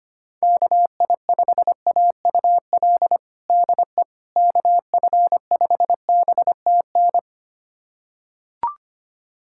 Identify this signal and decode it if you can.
Morse code, heard as a keyed tone, about 25 wpm, 700 Hz. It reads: KI5AUL DE KF5BTN
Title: USA Callsign DE Callsign - 6 Characters - 25wpm